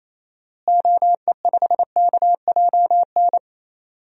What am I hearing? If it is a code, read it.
Morse code, heard as a keyed tone, about 28 wpm, 700 Hz. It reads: OE5KJN
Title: DXCC-100 Call Signs - 6 Characters - Repeated 3x - 28wpm